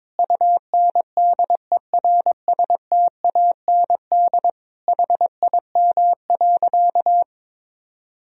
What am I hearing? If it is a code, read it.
Morse code, heard as a keyed tone, about 22 wpm, 700 Hz. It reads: UNDERSTAND HIM.
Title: Sentences from Top 1000 Words - Review Part 1 - 22wpm